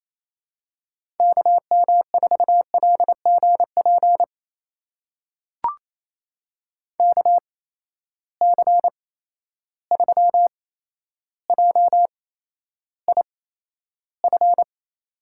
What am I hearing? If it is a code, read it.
Morse code, heard as a keyed tone, about 28 wpm, spaced as for 12 wpm, 700 Hz. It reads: KM4LGP KC3JIF
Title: USA Call Signs - 6 Characters - 8x Character Spacing - 28wpm